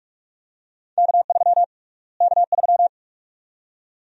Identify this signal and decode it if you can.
Morse code, heard as a keyed tone, about 46 wpm, 690 Hz. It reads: K3 K3
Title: USA Call Sign Prefixes - Repeated 2x - 45wpm